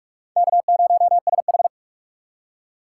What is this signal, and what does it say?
Morse code, heard as a keyed tone, about 43 wpm, 700 Hz. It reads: K0SH
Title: USA Call Signs - 4 Characters - 3x Character Spacing - 45wpm